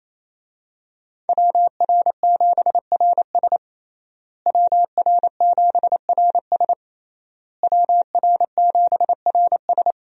Morse code, heard as a keyed tone, about 28 wpm, 700 Hz. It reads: WR7RS WR7RS WR7RS
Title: USA Call Signs - 5 Characters - Repeated 3x - 28wpm